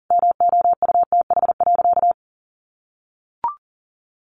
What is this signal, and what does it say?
Morse code, heard as a keyed tone, about 40 wpm, 700 Hz. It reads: MOUTH.
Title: Sentences from Top 1500 Words 40wpm